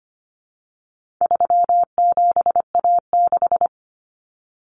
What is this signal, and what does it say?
Morse code, heard as a keyed tone, about 25 wpm, 700 Hz. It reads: 37A6
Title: Common Cut Numbers - 4-Digit 25wpm